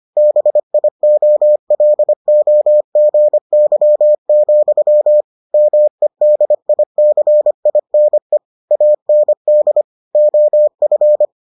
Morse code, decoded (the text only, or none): BIOLOGY, MEDICINE AND OF